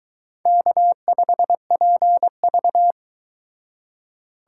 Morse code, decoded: K5PV

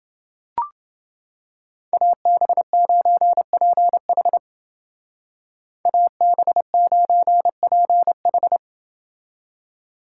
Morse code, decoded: AB9PH AB9PH

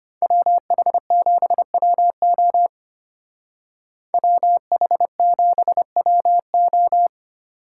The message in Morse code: WH7WO WH7WO

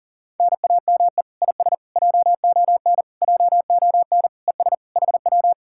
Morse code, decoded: NAME IS JON JON ES HW

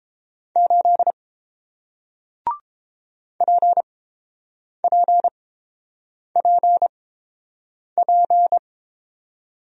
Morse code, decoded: 8 PPPP

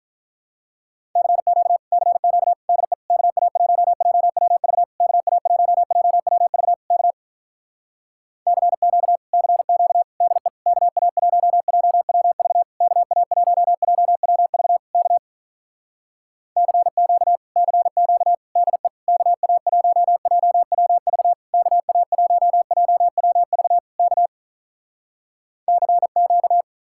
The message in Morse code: CQ CQ DE KA1JWV KA1JWV K CQ CQ DE KA1JWV KA1JWV K CQ CQ DE KA1JWV KA1JWV K CQ

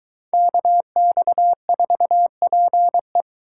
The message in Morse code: KX4PE

